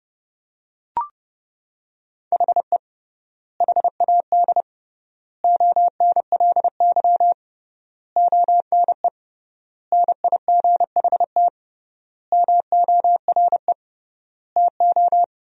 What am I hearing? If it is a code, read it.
Morse code, heard as a keyed tone, about 30 wpm, 700 Hz. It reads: HE HAD ONLY ONE NIGHT MORE TO
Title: 3x Word-Spacing - Sentences from Top 300 Words - 30wpm